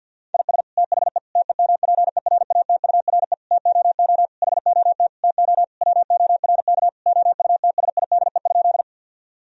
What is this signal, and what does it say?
Morse code, heard as a keyed tone, about 51 wpm, 700 Hz. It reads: IS THE TEMPERATURE TOO HOT TO WORK OUTSIDE?
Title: Sentences from Top 1000 Words - Review Part 4 - 50wpm